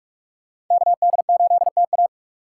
Morse code, heard as a keyed tone, about 40 wpm, 700 Hz. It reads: KD8TA